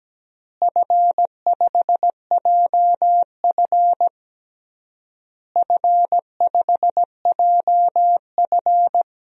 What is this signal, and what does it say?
Morse code, heard as a keyed tone, about 17 wpm, 700 Hz. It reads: F5JF F5JF
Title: DXCC-100 Call Signs - 4 Characters - Repeated 3x - 17wpm